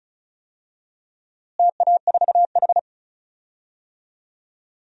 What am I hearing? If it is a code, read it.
Morse code, heard as a keyed tone, about 36 wpm, 700 Hz. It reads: TA4H
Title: DXCC-100 Call Signs - 4 Characters - Repeated 3x - 35wpm